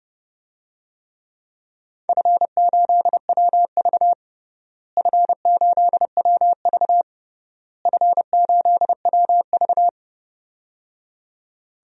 Morse code, heard as a keyed tone, about 30 wpm, 700 Hz. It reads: F8WV F8WV F8WV
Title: DXCC-100 Call Signs - 4 Characters - Repeated 3x - 30wpm